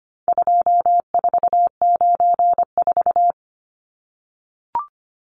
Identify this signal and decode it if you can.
Morse code, heard as a keyed tone, about 25 wpm, 700 Hz. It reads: 2494
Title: Numbers - 4-Digit 25wpm